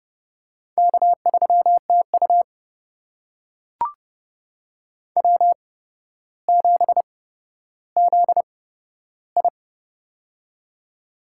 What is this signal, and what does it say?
Morse code, heard as a keyed tone, about 30 wpm, spaced as for 13 wpm, 700 Hz. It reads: K3TU W7ZI
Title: USA Call Signs - 4 Characters - 8x Character Spacing - 30wpm